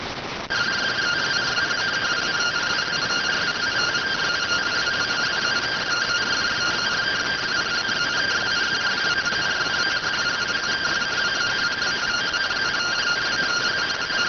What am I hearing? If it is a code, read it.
Two-tone teleprinter signal, about 170 Hz shift.